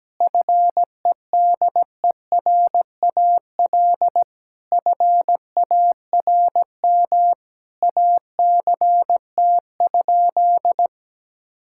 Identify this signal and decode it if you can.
Morse code, heard as a keyed tone, about 17 wpm, 700 Hz. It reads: FEDERAL FARM ACT?
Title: Sentences from Top 1000 Words - Review Part 5 - 17wpm